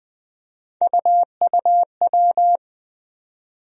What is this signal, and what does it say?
Morse code, heard as a keyed tone, about 20 wpm, 700 Hz. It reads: UUW
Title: USA Call Sign Postfixes - 3 Letters 20wpm